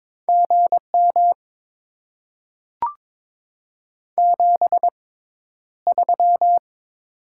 Morse code, heard as a keyed tone, about 22 wpm, spaced as for 11 wpm, 700 Hz. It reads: GM 73
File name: Top 20 QSO Elements - 6x Character Spacing - 22wpm